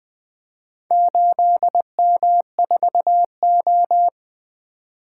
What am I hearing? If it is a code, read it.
Morse code, heard as a keyed tone, about 21 wpm, 700 Hz. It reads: 8M4O
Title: Four Letter-Number Combinations - 20wpm